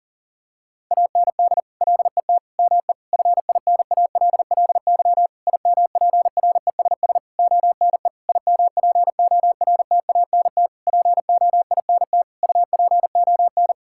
Morse code, heard as a keyed tone, about 40 wpm, 700 Hz. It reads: AND LET ME FINALLY IMPRESS ONE IMPORTANT POINT UPON
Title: Sentences from Top 1600 Words 40wpm